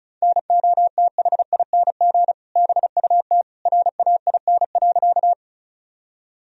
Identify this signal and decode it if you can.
Morse code, heard as a keyed tone, about 36 wpm, 700 Hz. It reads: NOTHING BUT RAIN.
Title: Sentences from Top 1000 Words - Review Part 1 - 35wpm